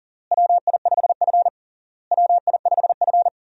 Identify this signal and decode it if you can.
Morse code, heard as a keyed tone, about 40 wpm, 700 Hz. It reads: WI5F WI5F
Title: USA Call Signs - 4 Characters - Repeated 3x - 40wpm